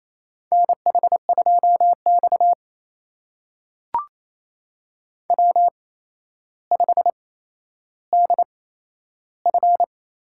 Morse code, heard as a keyed tone, about 28 wpm, spaced as for 12 wpm, 700 Hz. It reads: NH2X W5DF
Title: USA Call Signs - 4 Characters - 8x Character Spacing - 28wpm